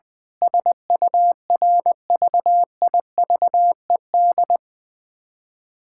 Morse code, heard as a keyed tone, about 19 wpm, 700 Hz. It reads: SURVIVED
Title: Sentences with 7-Word Phrases - Spoken Beginning - 20wpm